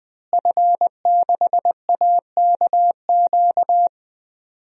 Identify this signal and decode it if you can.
Morse code, heard as a keyed tone, about 21 wpm, 700 Hz. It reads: F6AKQ
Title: DXCC-100 Call Signs - 5 Characters - Repeated 3x - 20wpm